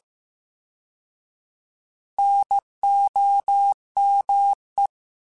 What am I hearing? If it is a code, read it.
Morse code, heard as a keyed tone, about 15 wpm, 800 Hz. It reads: NOME